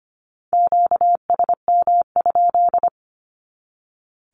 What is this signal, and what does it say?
Morse code, heard as a keyed tone, about 25 wpm, 700 Hz. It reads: QSM?
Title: Q-Codes 25wpm